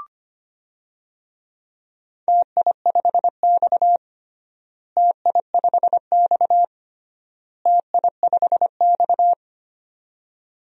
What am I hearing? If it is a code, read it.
Morse code, heard as a keyed tone, about 25 wpm, 700 Hz. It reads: TI5X TI5X TI5X
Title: DXCC-100 Call Signs - 4 Characters - Repeated 3x - 25wpm